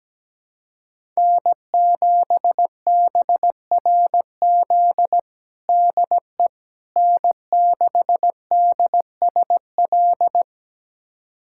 Morse code, heard as a keyed tone, about 17 wpm, 700 Hz. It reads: N7BRZ DE N6DSL
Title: USA Callsign DE Callsign - 5 Characters - 17wpm